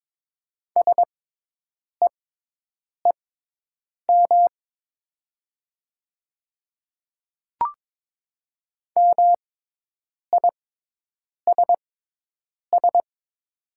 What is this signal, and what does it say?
Morse code, heard as a keyed tone, about 22 wpm, spaced as for 8 wpm, 700 Hz. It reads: SEEM MISS
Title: 6x Character Spacing - Top 500 Words - 22wpm